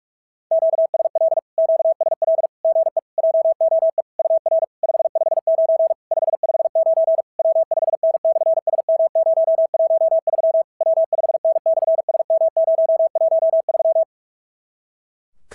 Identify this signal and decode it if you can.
Morse code, heard as a keyed tone, about 46 wpm, 650 Hz. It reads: QSL QSL GE JOE UR 559 559 W5N/SM013 W5N/SM013
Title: SOTA QSO Practice 45wpm